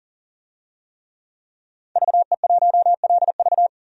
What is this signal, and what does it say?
Morse code, heard as a keyed tone, about 40 wpm, 700 Hz. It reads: VE1LV
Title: DXCC-100 Call Signs - 5 Characters - 40wpm